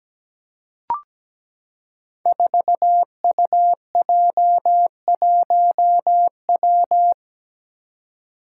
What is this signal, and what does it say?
Morse code, heard as a keyed tone, about 17 wpm, 700 Hz. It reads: 4UJ1W
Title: Five Challenging Letter-Number Combinations 17wpm